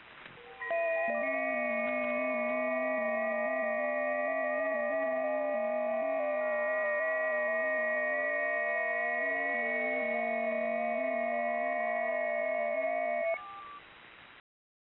FT8.